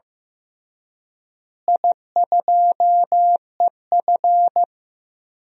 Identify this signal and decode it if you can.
Morse code, heard as a keyed tone, about 16 wpm, 700 Hz. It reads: I2EF
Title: DXCC-100 Call Signs - 4 Characters - Repeated 3x - 15wpm